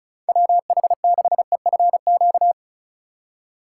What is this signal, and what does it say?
Morse code, heard as a keyed tone, about 34 wpm, 700 Hz. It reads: WH6EFQ
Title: USA Call Signs - 6 Characters - 2x Character Spacing - 35wpm